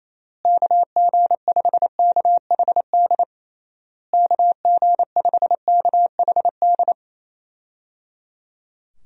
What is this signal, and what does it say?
Morse code, heard as a keyed tone, about 28 wpm, 700 Hz. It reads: KG5KHD KG5KHD
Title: USA Call Signs - 6 Characters - Repeated 3x - 28wpm